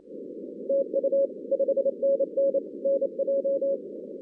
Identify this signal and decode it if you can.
Morse code, heard as a keyed tone, about 27 wpm, 550 Hz, with noise. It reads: TU 5NN NJ
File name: 2016 arrl dx cw / 2016-ARRL-DX-CW-K5TR- - 0-46